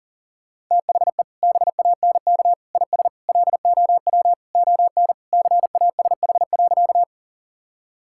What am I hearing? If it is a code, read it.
Morse code, heard as a keyed tone, about 40 wpm, 700 Hz. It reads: THE BANK IS LOW ON CASH.
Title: Sentences from Top 1000 Words - Review Part 5 - 40wpm